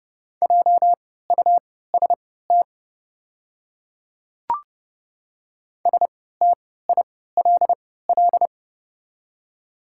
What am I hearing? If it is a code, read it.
Morse code, heard as a keyed tone, about 30 wpm, spaced as for 17 wpm, 700 Hz. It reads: JUST STILL
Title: 3x Character Spacing - Top 300 Words - 30wpm